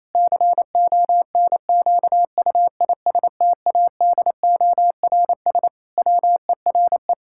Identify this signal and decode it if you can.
Morse code, heard as a keyed tone, about 28 wpm, 700 Hz. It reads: CONQUISTADORS WERE
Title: Pangram Challenge v2 - 28wpm